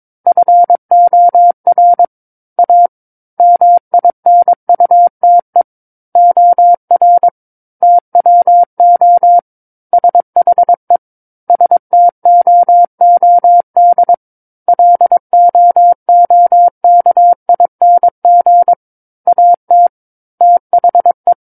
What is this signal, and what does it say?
Morse code, heard as a keyed tone, about 22 wpm, 700 Hz. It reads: FOR A MINUTE OR TWO SHE STOOD LOOKING AT THE
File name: 06-Chapter-06 22 WPM